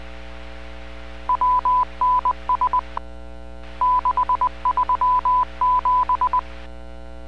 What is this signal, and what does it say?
Morse code, heard as a keyed tone, about 20 wpm, 1000 Hz. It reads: WNS 637